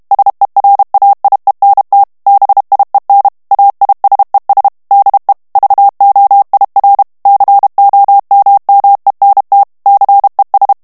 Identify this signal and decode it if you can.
Morse code, heard as a keyed tone, about 32 wpm, 800 Hz. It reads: SERAIENT BIEN AISES DE VOIR COMMENT CES
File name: FRANCAIS TEXTE CLAIR 2 VITESSE 30 MOTS MINUTES